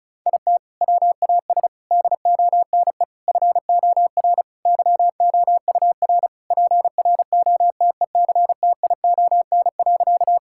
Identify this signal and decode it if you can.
Morse code, heard as a keyed tone, about 36 wpm, 700 Hz. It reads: IT WAS DONE FOR YOUR PROTECTION.
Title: Sentences from Top 1600 Words 35wpm